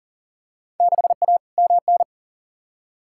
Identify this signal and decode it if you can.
Morse code, heard as a keyed tone, about 40 wpm, 700 Hz. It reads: 6A MN
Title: Field Day Exchanges Encore - 40wpm